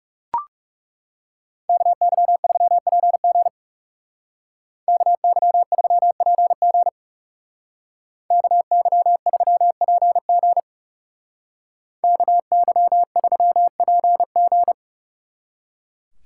Morse code, heard as a keyed tone, about 35 wpm, 700 Hz. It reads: KY3PG KY3PG KY3PG KY3PG
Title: Speed Racing - USA 2x2 Call Signs 30wpm